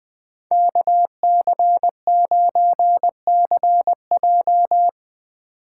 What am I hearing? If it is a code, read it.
Morse code, heard as a keyed tone, about 20 wpm, 700 Hz. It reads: KC9CJ